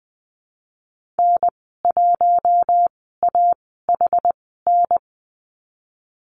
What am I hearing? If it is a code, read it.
Morse code, heard as a keyed tone, about 20 wpm, spaced as for 14 wpm, 700 Hz. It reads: N1AHN
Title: 076 Lesson 13g - Letter L - Call Signs 20wpm